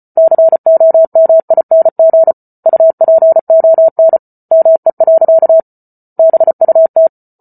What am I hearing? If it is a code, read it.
Morse code, heard as a keyed tone, about 34 wpm, 650 Hz. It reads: COMING UPON ME. BUT